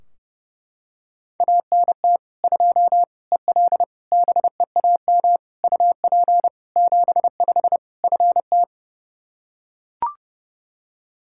Morse code, heard as a keyed tone, about 32 wpm, 700 Hz. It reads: ANT 2 EL BEAM UP 75 FT
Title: Practice QSO - Exchange 2 - Info - 30wpm